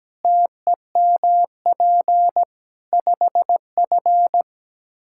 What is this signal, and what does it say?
Morse code, heard as a keyed tone, about 17 wpm, 700 Hz. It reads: TEMP 5F
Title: Practice QSO - Exchange 2 - Info - 17wpm